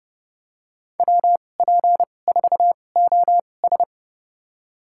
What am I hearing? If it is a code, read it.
Morse code, heard as a keyed tone, about 31 wpm, spaced as for 22 wpm, 700 Hz. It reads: WP4OS
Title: USA Call Signs - 5 Characters - 2x Character Spacing - 30wpm